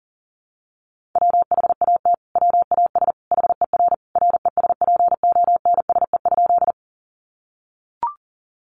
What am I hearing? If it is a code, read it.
Morse code, heard as a keyed tone, about 40 wpm, 700 Hz. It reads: WHAT WAS HER RESPONSE?
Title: Sentences from Top 1400 Words 40wpm